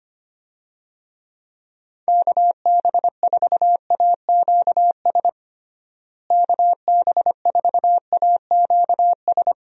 Morse code, heard as a keyed tone, about 25 wpm, 700 Hz. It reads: KB4AQS KB4AQS
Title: USA Call Signs - 6 Characters - Repeated 2x - 25wpm